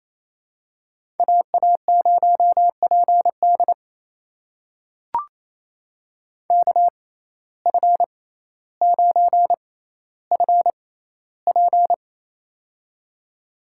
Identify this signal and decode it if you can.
Morse code, heard as a keyed tone, about 28 wpm, spaced as for 14 wpm, 700 Hz. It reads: AA0PD KF9FP
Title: USA Call Signs - 5 Characters - 6x Character Spacing - 28wpm